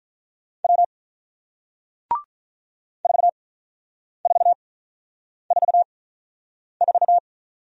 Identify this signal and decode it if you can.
Morse code, heard as a keyed tone, about 47 wpm, spaced as for 9 wpm, 700 Hz. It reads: W 4444